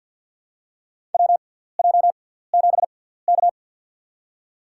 Morse code, heard as a keyed tone, about 47 wpm, spaced as for 17 wpm, 700 Hz. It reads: WJ7K
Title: USA Call Signs - 4 Characters - 6x Character Spacing - 50wpm